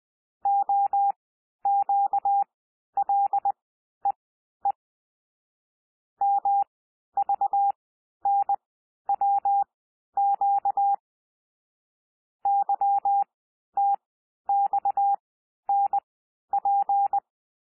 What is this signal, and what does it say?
Morse code, heard as a keyed tone, about 20 wpm, spaced as for 11 wpm, 800 Hz. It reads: OQLEE MVNWQ YTXNP
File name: cwmp3 / L3 SOUND LETTERS 20 wpm